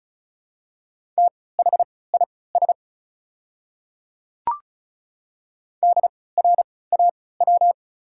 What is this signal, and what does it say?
Morse code, heard as a keyed tone, about 34 wpm, spaced as for 20 wpm, 700 Hz. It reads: THIS DRAW
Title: 3x Character Spacing - Top 300 Words - 35wpm